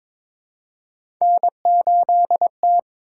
Morse code, heard as a keyed tone, about 22 wpm, 700 Hz. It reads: N8T
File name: practice / 22wpm / Three Letter-Number Combinations - 22wpm